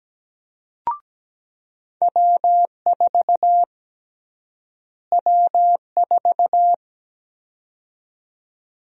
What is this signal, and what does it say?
Morse code, heard as a keyed tone, about 17 wpm, 700 Hz. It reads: W4 W4
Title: USA Call Sign Prefixes - Repeated 2x - 17wpm